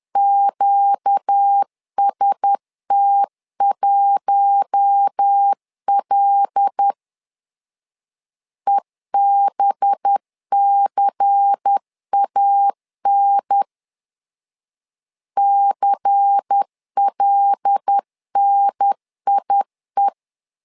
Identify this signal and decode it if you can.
Morse code, heard as a keyed tone, about 10 wpm, 800 Hz. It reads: QST1L EBCAN CLNIE
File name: LEZIONI / 5 LBVC / 5 LBVC 10wpm 2349